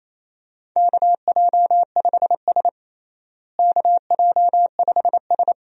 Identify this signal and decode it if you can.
Morse code, heard as a keyed tone, about 28 wpm, 700 Hz. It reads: KJ5S KJ5S
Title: USA Call Signs - 4 Characters - Repeated 2x - 28wpm